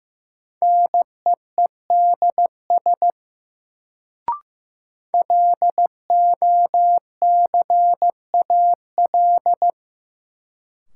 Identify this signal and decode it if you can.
Morse code, heard as a keyed tone, about 15 wpm, 700 Hz. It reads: NEEDS LOCAL
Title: Sentences from Top 1000 Words - Review Part 3 - New Word Drill - 15wpm